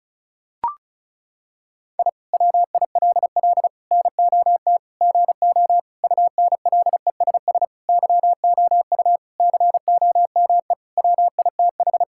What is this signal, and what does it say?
Morse code, heard as a keyed tone, about 36 wpm, 700 Hz. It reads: I WILL NOT GO UNLESS YOU COME WITH
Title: Sentences from Top 1000 Words - Review Part 6 - 35wpm